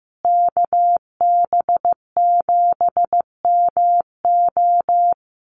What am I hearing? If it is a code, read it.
Morse code, heard as a keyed tone, about 15 wpm, 700 Hz. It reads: KB7MO
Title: USA Call Signs Repeated 2x 15wpm